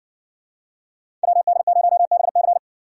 Morse code, heard as a keyed tone, about 56 wpm, 700 Hz. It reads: WD0BC